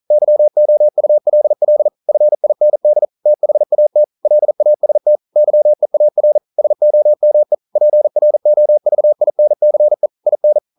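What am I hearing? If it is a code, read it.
Morse code, heard as a keyed tone, about 40 wpm, 600 Hz. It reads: YOULL FIND THAT LAST YEAR SOME PROVINCE IN
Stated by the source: Morse code practice at 41 wpm, from CW med Gnister.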